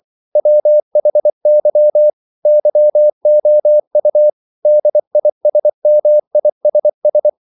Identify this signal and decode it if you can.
Morse code, heard as a keyed tone, about 24 wpm, 600 Hz. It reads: WHY YOU DISMISS
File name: CW / Practice / EnglishQuotations0424